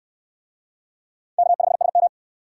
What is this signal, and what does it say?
Morse code, heard as a keyed tone, about 68 wpm, 700 Hz. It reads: K5SM